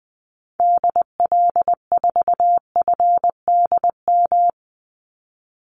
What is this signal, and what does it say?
Morse code, heard as a keyed tone, about 20 wpm, 700 Hz. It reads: DL4FDM
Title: Call Signs Repeated 2x - 20wpm